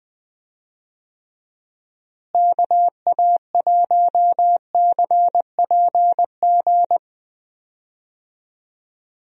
Morse code, heard as a keyed tone, about 20 wpm, 700 Hz. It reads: KA1CPG
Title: USA Call Signs - 6 Characters - Repeated 2x - 20wpm